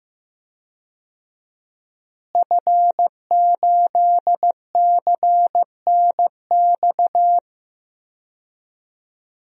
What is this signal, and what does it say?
Morse code, heard as a keyed tone, about 15 wpm, 700 Hz. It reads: F8CNX
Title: DXCC-100 Call Signs - 5 Characters - Repeated 2x - 15wpm